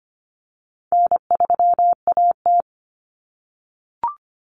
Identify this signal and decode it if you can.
Morse code, heard as a keyed tone, about 25 wpm, 700 Hz. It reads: N3AT
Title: Common Cut Numbers - 4-Digit 25wpm